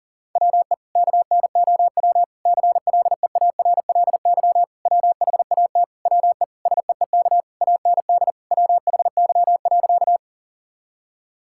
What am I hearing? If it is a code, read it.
Morse code, heard as a keyed tone, about 38 wpm, 700 Hz. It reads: WE KNOW CLEARLY WHAT WE SEEK AND WHY.
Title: Sentences from Top 1600 Words 40wpm